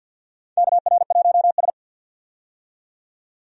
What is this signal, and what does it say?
Morse code, heard as a keyed tone, about 48 wpm, 700 Hz. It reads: KR1S
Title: USA Call Signs - 4 Characters - 50wpm